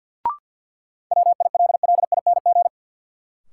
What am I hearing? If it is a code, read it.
Morse code, heard as a keyed tone, about 52 wpm, 700 Hz. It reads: WILLING
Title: Sentences from Top 1000 Words - New Word Drill - 50wpm